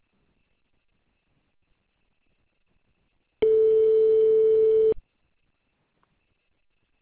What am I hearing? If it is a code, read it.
Ringback tone (France), 440 Hz.